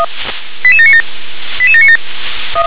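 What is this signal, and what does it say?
Five-tone selective call (ZVEI), heard as quick runs of five tones, tones of about 70 ms. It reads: 88797 88797
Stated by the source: ZVEI five-tone selective call 88768 received on an SDR.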